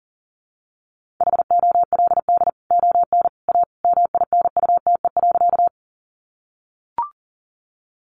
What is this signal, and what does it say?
Morse code, heard as a keyed tone, about 40 wpm, 700 Hz. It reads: HOLD ON A MINUTE.
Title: Sentences from Top 500 Words 40wpm